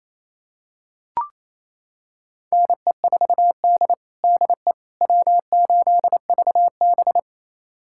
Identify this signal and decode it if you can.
Morse code, heard as a keyed tone, about 28 wpm, 700 Hz. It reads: NE4D DE W8VB
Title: USA Callsign DE Callsign - 4 Characters - 28wpm